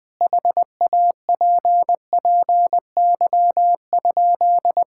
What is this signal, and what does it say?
Morse code, heard as a keyed tone, about 21 wpm, 700 Hz. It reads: HAPPY?
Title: Sentences from Top 1000 Words - Review Part 5 - 20wpm